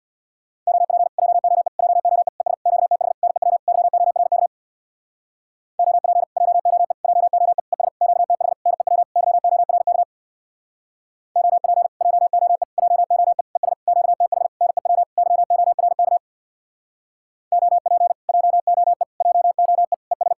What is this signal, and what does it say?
Morse code, heard as a keyed tone, about 60 wpm, 700 Hz. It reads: OP JOE JOE ES QTH NEW YORK OP JOE JOE ES QTH NEW YORK OP JOE JOE ES QTH NEW YORK OP JOE JOE ES
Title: Speed-Racing Common QSO Phrases Encore - 50wpm